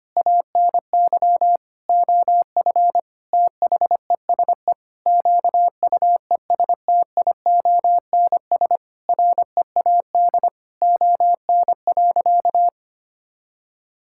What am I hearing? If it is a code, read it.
Morse code, heard as a keyed tone, about 25 wpm, 700 Hz. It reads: ANY OF THESE QUESTIONS READ ON.
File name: practice / 25wpm / Sentences with 9-Word Phrases - 25wpm